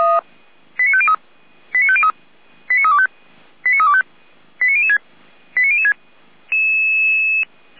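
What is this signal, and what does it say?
Five-tone selective call (ZVEI), heard as quick runs of five tones, tones of about 70 ms. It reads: 89503 89503 89325 89325 89006 89006
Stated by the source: ZVEI five-tone selective call received by SDR, on the Feuerwehr Orlamünde fire brigade audio server.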